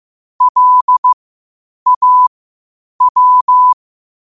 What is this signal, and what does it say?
Morse code, heard as a keyed tone, about 15 wpm, spaced as for 8 wpm, 1000 Hz. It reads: LAW
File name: qotdfiles / 8wpm